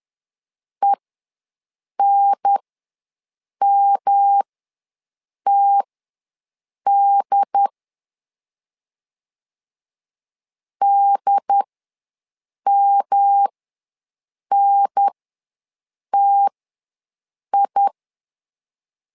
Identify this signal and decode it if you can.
Morse code, heard as a keyed tone, about 10 wpm, spaced as for 6 wpm, 800 Hz. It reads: ENMTD DMNTI
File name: LEZIONI / 3 MWRO / 3 M ws 10wpm 2121